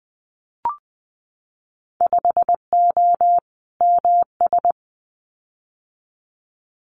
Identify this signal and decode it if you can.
Morse code, heard as a keyed tone, about 20 wpm, 700 Hz. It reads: 5O MS
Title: Winter Field Day Exchanges Encore 20wpm